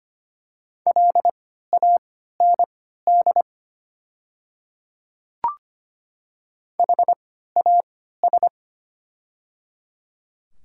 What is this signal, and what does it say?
Morse code, heard as a keyed tone, about 25 wpm, spaced as for 14 wpm, 690 Hz. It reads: LAND HAS
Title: 3x Character Spacing - Top 300 Words - 25wpm